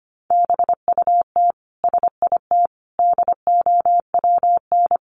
Morse code, heard as a keyed tone, about 25 wpm, 700 Hz. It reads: BUT SIT DOWN
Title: Sentences from Top 600 Words 25wpm